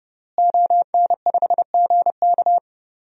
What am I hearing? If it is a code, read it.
Morse code, heard as a keyed tone, about 30 wpm, 700 Hz. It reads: ON5GK
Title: DXCC-100 Call Signs - 5 Characters - Repeated 3x - 30wpm